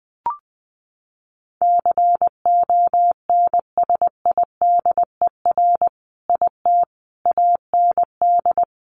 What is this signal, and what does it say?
Morse code, heard as a keyed tone, about 20 wpm, 700 Hz. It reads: CONSIDER IT AND
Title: Sentences from Top 1400 Words 20wpm